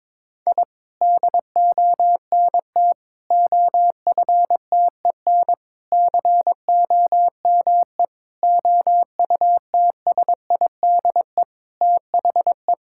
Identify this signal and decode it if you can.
Morse code, heard as a keyed tone, about 22 wpm, 700 Hz. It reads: I DONT OFTEN COME OUTSIDE THE
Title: Sentences from Top 1000 Words - Review Part 6 - 22wpm